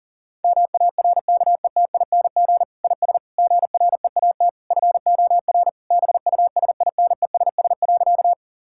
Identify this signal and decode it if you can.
Morse code, heard as a keyed tone, about 40 wpm, 700 Hz. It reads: MARKETING IS GREAT FOR BUSINESS.
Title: Sentences from Top 1000 Words - Review Part 5 - 40wpm